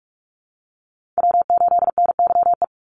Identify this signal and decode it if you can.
Morse code, heard as a keyed tone, about 44 wpm, 700 Hz. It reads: W8NYE
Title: USA Call Signs 45wpm